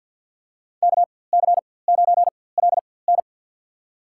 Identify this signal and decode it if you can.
Morse code, heard as a keyed tone, about 53 wpm, spaced as for 24 wpm, 700 Hz. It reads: KC9LN
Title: USA Call Signs - 5 Characters - 4x Character Spacing - 50wpm